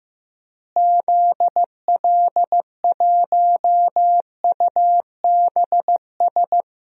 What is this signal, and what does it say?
Morse code, heard as a keyed tone, about 15 wpm, 700 Hz. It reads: ZL1UBS
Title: DXCC-100 Call Signs - 6 Characters - Repeated 3x - 15wpm